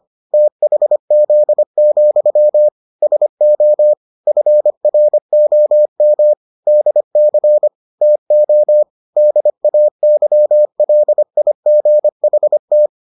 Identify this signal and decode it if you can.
Morse code, heard as a keyed tone, about 25 wpm, 600 Hz. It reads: THZ, SO FROM DC TO DAYLIGHT